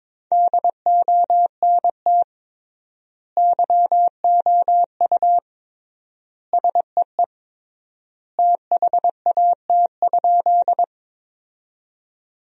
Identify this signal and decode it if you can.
Morse code, heard as a keyed tone, about 22 wpm, 700 Hz. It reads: DONT YOU SEE THAT?
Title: 3x Word-Spacing - Sentences from Top 300 Words - 22wpm